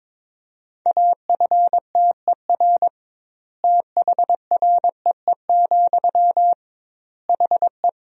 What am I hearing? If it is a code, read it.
Morse code, heard as a keyed tone, about 22 wpm, 700 Hz. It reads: AFTER THREE, HE
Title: 2x Word-Spacing - Sentences from Top 200 Words - 22wpm